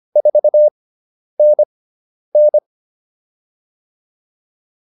Morse code, heard as a keyed tone, about 25 wpm, spaced as for 10 wpm, 600 Hz. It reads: 4NN